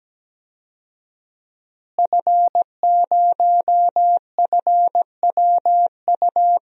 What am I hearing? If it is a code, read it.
Morse code, heard as a keyed tone, about 17 wpm, 700 Hz. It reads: F0FWU
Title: DXCC-100 Call Signs - 5 Characters - Repeated 2x - 17wpm